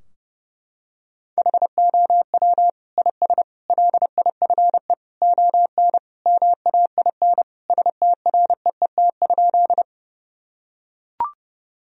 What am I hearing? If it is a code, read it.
Morse code, heard as a keyed tone, about 32 wpm, 700 Hz. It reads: HOW IS LIFE ON MAIN STREET?
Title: Sentences from Top 1000 Words - Review Part 2 - 30wpm